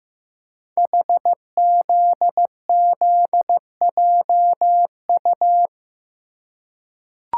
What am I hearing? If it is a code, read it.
Morse code, heard as a keyed tone, about 15 wpm, 700 Hz. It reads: HZZJU
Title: Code Groups - 5 Letters - 15wpm